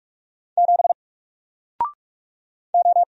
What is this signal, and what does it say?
Morse code, heard as a keyed tone, about 45 wpm, 700 Hz. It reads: 7 O
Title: Single Letter-Number Alt 45wpm